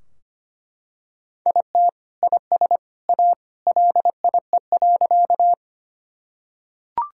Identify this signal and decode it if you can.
Morse code, heard as a keyed tone, about 25 wpm, 700 Hz. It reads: IT IS A LIE.
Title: Sentences from Top 1600 Words 25wpm